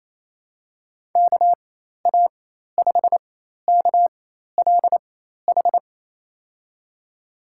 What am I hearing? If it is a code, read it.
Morse code, heard as a keyed tone, about 28 wpm, spaced as for 13 wpm, 700 Hz. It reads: KA5KLH
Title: USA Call Signs - 6 Characters - 4x Character Spacing - 28wpm